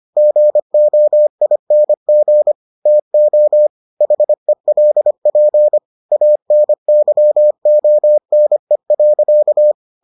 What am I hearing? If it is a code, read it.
Morse code, heard as a keyed tone, about 25 wpm, 600 Hz. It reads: GOING TO HELP ANYONE.